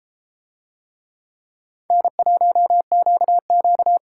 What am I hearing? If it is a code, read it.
Morse code, heard as a keyed tone, about 32 wpm, 700 Hz. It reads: N1QQ